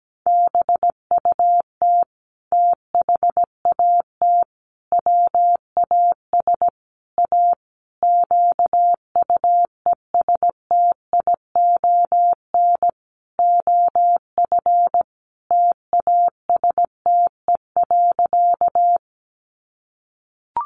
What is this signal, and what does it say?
Morse code, heard as a keyed tone, about 17 wpm, 700 Hz. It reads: BUT THAT WAS A QUESTION OF TASTE.
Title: Sentences from Top 1100 Words 17wpm